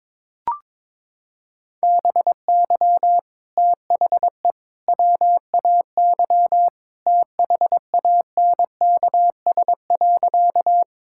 Morse code, heard as a keyed tone, about 22 wpm, 700 Hz. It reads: BY THE WAY THANKS.
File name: practice / 22wpm / Sentences with 3-Word Phrases - 22wpm